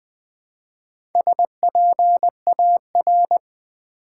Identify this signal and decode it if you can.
Morse code, heard as a keyed tone, about 20 wpm, 700 Hz. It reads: SPAR
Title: Build A Word - 1 Letter At A Time - 5-Letter Words - 20wpm